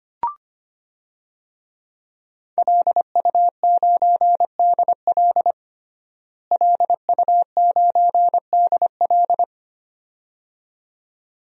Morse code, heard as a keyed tone, about 25 wpm, 700 Hz. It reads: LU9DL LU9DL
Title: DXCC-100 Call Signs - 5 Characters - Repeated 2x - 25wpm